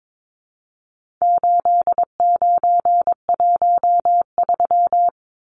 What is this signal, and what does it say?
Morse code, heard as a keyed tone, about 22 wpm, 700 Hz. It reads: 8913
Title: Numbers - 4-Digit 22wpm